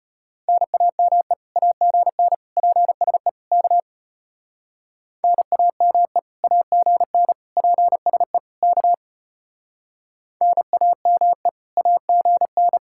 Morse code, heard as a keyed tone, about 31 wpm, 700 Hz. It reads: NAME AGN PSE K NAME AGN PSE K NAME AGN
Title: Speed-Racing Common QSO Phrases Encore - 25wpm